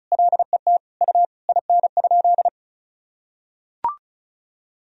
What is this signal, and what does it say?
Morse code, heard as a keyed tone, about 34 wpm, 700 Hz. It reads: LET U IN?
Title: Sentences from Top 300 Words and Abbreviations 35wpm